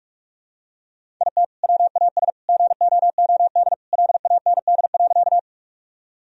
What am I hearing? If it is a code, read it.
Morse code, heard as a keyed tone, about 46 wpm, 700 Hz. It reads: IT WAS GOOD LAND.
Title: Sentences from Top 1000 Words - Review Part 1 - 45wpm